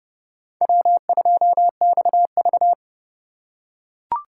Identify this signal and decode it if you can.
Morse code, heard as a keyed tone, about 30 wpm, 700 Hz. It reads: W2XV
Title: USA Call Signs - 4x Character Spacing - 30wpm